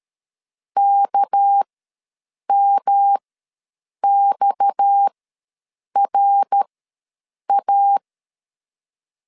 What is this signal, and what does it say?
Morse code, heard as a keyed tone, about 13 wpm, spaced as for 7 wpm, 800 Hz. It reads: KMXRA
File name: LEZIONI / 6 PKX0 / 6 PKX ws 12wpm 3106